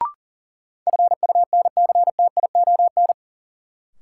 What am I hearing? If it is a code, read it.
Morse code, heard as a keyed tone, about 40 wpm, 700 Hz. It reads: FUNCTION
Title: Sentences from Top 1100 Words - New Word Drill Alt - 40wpm